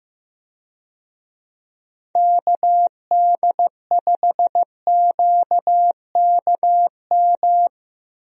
Morse code, heard as a keyed tone, about 15 wpm, 700 Hz. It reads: KD5QKM
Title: USA Call Signs - 6 Characters - Repeated 3x - 15wpm